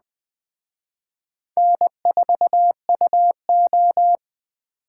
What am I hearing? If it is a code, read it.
Morse code, heard as a keyed tone, about 19 wpm, 700 Hz. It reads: N4UO